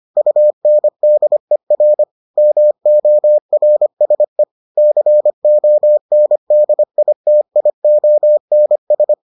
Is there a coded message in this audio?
Yes: Morse code from 0.2 to 9.1 s, heard as a keyed tone, about 25 wpm, 600 Hz. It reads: UNDER MORSE CONDITIONS